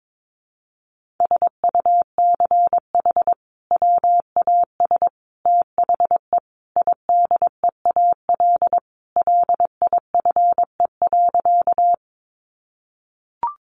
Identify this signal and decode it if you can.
Morse code, heard as a keyed tone, about 22 wpm, 700 Hz. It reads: SUCH WAS THE IDEAL LIFE.
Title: Sentences from Top 1400 Words 22wpm